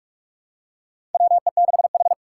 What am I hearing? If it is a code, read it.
Morse code, heard as a keyed tone, about 46 wpm, 700 Hz. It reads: WE6H
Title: USA Call Signs - 4 Characters - 3x Character Spacing - 45wpm